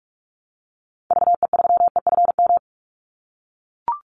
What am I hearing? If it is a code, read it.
Morse code, heard as a keyed tone, about 44 wpm, 700 Hz. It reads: VE3EFM
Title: Call Signs - 45wpm